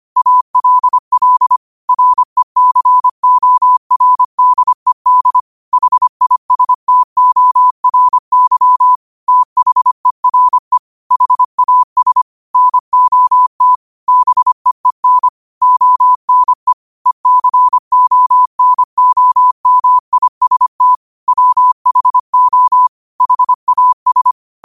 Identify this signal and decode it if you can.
Morse code, heard as a keyed tone, about 25 wpm, 1000 Hz. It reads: ALL RECORDED HISTORY THERE HAS NOT BEEN ONE ECONOMIST WHO HAS
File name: qotdfiles / 25wpm